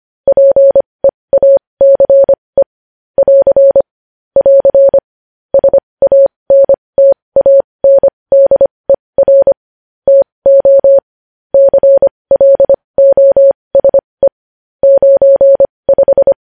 Morse code, heard as a keyed tone, about 25 wpm, spaced as for 20 wpm, 550 Hz. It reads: PEACE <AR> <AR> SANTANDER TO CLOSE 95
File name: podcast / News headlines-1x-25x20